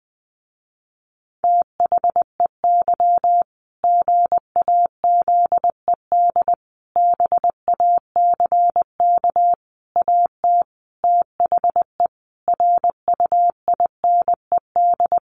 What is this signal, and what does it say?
Morse code, heard as a keyed tone, about 20 wpm, 700 Hz. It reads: THEY GAZED BACK AT THE RUINED
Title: A Sound Of Thunder 20wpm